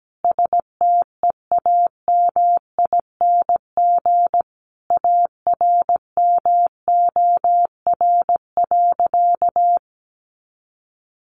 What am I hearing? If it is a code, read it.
Morse code, heard as a keyed tone, about 17 wpm, 700 Hz. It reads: STEAMING ARMOR.
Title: A Sound Of Thunder 17wpm